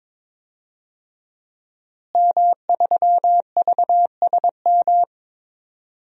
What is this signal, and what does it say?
Morse code, heard as a keyed tone, about 22 wpm, 700 Hz. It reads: M3VSM
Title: DXCC-100 Call Signs - 5 Characters - Repeated 3x - 22wpm